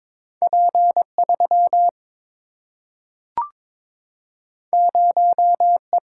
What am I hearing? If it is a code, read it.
Morse code, heard as a keyed tone, about 22 wpm, 700 Hz. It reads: P 3 0 E
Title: Single-Letter Single-Number 22wpm